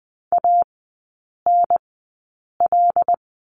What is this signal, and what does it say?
Morse code, heard as a keyed tone, about 20 wpm, spaced as for 8 wpm, 700 Hz. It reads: ANL
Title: 074 Lesson 13e - Letter L - Three Characters 20wpm